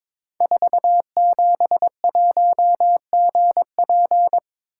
Morse code, heard as a keyed tone, about 22 wpm, 700 Hz. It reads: 471GP
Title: Code Groups - 5 Letter:Numbers - 22wpm